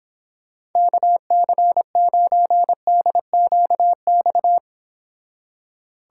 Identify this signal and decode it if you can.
Morse code, heard as a keyed tone, about 26 wpm, 700 Hz. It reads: KC9DQX